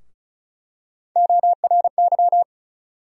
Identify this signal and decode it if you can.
Morse code, heard as a keyed tone, about 34 wpm, 700 Hz. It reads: ORY
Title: USA Call Sign Postfixes - 3 Letters 35wpm